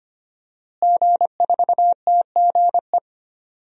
Morse code, heard as a keyed tone, about 25 wpm, 700 Hz. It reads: G4TGE
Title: DXCC-100 Call Signs - 5 Characters - Repeated 3x - 25wpm